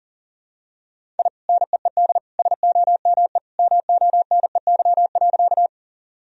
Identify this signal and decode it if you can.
Morse code, heard as a keyed tone, about 37 wpm, 700 Hz. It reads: I NEED SOME MONEY.
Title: Sentences from Top 1000 Words - Review Part 1 - 40wpm